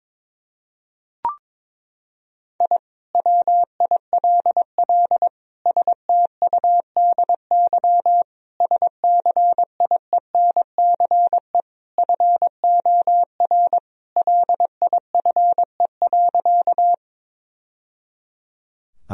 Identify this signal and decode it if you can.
Morse code, heard as a keyed tone, about 22 wpm, 700 Hz. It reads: I WILL STUDY SCIENCE FOR LIFE.
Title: Sentences from Top 1000 Words - Review Part 2 - 22wpm